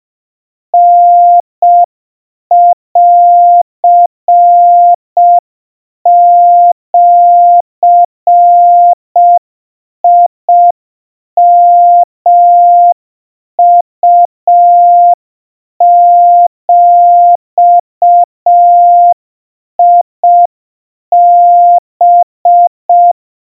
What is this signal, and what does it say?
Morse code (Japanese Wabun code), heard as a keyed tone, about 5 wpm, 700 Hz. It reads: タンジヨウビハ
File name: 20 cpm 700 Hz Crispy